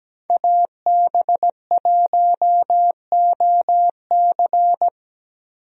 Morse code, heard as a keyed tone, about 17 wpm, 700 Hz. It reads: AB1OC
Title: USA Call Signs - 5 Characters - Repeated 3x - 17wpm